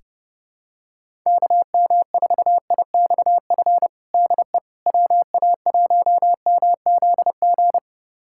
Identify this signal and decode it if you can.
Morse code, heard as a keyed tone, about 30 wpm, 700 Hz. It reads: KM4IXF DE WA1MZG
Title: USA Callsign DE Callsign 30wpm